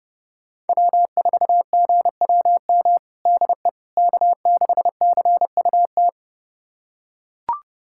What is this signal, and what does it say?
Morse code, heard as a keyed tone, about 30 wpm, 700 Hz. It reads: W4GWM DE K6CUT
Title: USA Callsign DE Callsign - 5 Characters - 30wpm